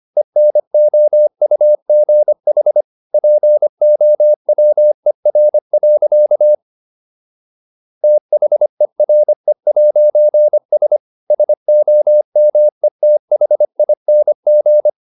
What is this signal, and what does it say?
Morse code, heard as a keyed tone, about 25 wpm, 600 Hz. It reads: ENOUGH POWER. THERE'S SOMETHING